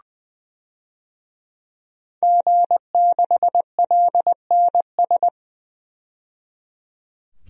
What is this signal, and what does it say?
Morse code, heard as a keyed tone, about 20 wpm, 700 Hz. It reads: G6LNS
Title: DXCC-100 Call Signs - 5 Characters - 20wpm